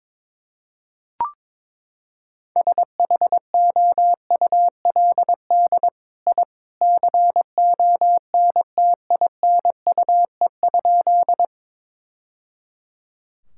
Morse code, heard as a keyed tone, about 22 wpm, 700 Hz. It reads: SHOULD I CONTINUE?
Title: Sentences from Top 1000 Words - Review Part 6 - 22wpm